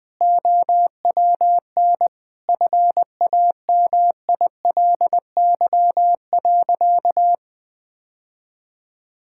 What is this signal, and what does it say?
Morse code, heard as a keyed tone, about 20 wpm, 700 Hz. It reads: OWN FAMILY.
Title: Sentences from Top 1000 Words - Review Part 5 - 20wpm